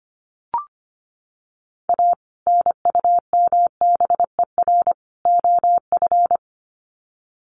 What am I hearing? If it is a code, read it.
Morse code, heard as a keyed tone, about 25 wpm, 700 Hz. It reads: A NUMBER OF
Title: Sets of 3 Words 25wpm